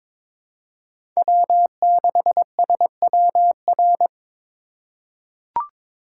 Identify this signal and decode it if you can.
Morse code, heard as a keyed tone, about 22 wpm, 700 Hz. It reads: W6SWR
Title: USA Call Signs - 5 Characters - 2x Character Spacing - 22wpm